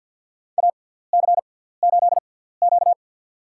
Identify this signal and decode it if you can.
Morse code, heard as a keyed tone, about 51 wpm, spaced as for 17 wpm, 700 Hz. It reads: AC8Q
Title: USA Call Signs - 6x Character Spacing - 50wpm